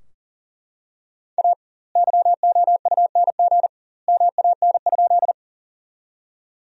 Morse code, heard as a keyed tone, about 40 wpm, 700 Hz. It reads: A YOUNG MAN?
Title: 2x Word-Spacing - Sentences from Top 400 Words - 40wpm